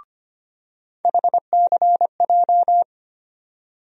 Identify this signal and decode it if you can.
Morse code, heard as a keyed tone, about 25 wpm, 700 Hz. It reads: HCJ
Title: USA Call Sign Postfixes - 3 Letters 25wpm